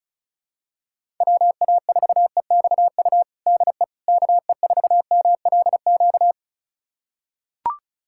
Morse code, heard as a keyed tone, about 36 wpm, 700 Hz. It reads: WA4EXU DE KE4MLQ
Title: USA Callsign DE Callsign - 6 Characters - 35wpm